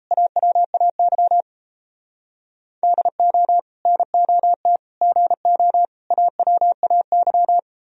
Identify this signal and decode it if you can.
Morse code, heard as a keyed tone, about 34 wpm, 700 Hz. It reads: AWAY DO NOT GO AWAY
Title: Speed-Racing - Sets of 4 Words 28wpm